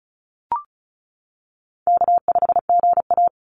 Morse code, heard as a keyed tone, about 34 wpm, 700 Hz. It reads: K5GA
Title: Call Signs - 35wpm